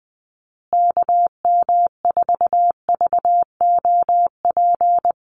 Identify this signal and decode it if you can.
Morse code, heard as a keyed tone, about 20 wpm, 700 Hz. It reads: KM4VOP
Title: USA Call Signs Repeated 2x 20wpm